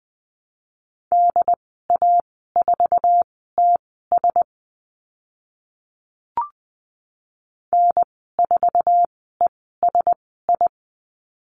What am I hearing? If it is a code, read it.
Morse code, heard as a keyed tone, about 20 wpm, spaced as for 14 wpm, 700 Hz. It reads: DA4TS N4ESI
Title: 069 Lesson 12g - Letter d - Call Signs 20wpm